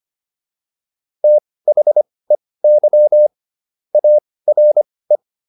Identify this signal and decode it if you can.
Morse code, heard as a keyed tone, about 25 wpm, spaced as for 18 wpm, 600 Hz. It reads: THEY ARE